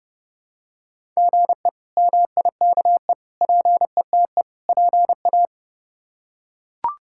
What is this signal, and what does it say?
Morse code, heard as a keyed tone, about 30 wpm, 700 Hz. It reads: GE MIKE PETE PA
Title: K1USN SST QSO Practice - 30wpm